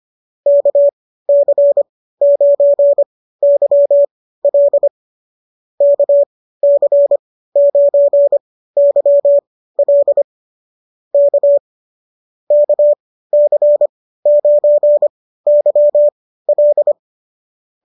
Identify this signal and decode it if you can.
Morse code, heard as a keyed tone, about 25 wpm, spaced as for 15 wpm, 590 Hz. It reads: KC9YL KC9YL K KC9YL